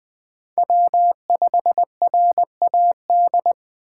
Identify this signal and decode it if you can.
Morse code, heard as a keyed tone, about 21 wpm, 700 Hz. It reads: W5RAD
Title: Speed Racing - USA 1x3 Call Signs 17wpm